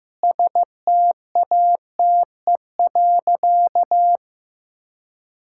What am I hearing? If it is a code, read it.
Morse code, heard as a keyed tone, about 15 wpm, 700 Hz. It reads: STATE.
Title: Sentences from Top 1000 Words - Review Part 5 - 15wpm